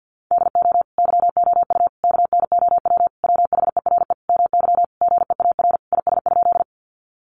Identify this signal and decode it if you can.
Morse code, heard as a keyed tone, about 50 wpm, 700 Hz. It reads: DO YOU KNOW WHERE MY GEAR IS?
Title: Sentences from Top 1400 Words 50wpm